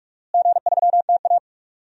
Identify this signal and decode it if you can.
Morse code, heard as a keyed tone, about 46 wpm, 700 Hz. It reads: G3TA
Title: DXCC-100 Call Signs - 4 Characters - Repeated 3x - 45wpm